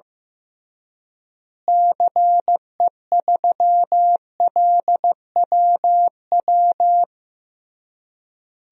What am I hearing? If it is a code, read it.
Morse code, heard as a keyed tone, about 15 wpm, 700 Hz. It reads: CE3LWW
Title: DXCC-100 Call Signs - 6 Characters - Repeated 3x - 15wpm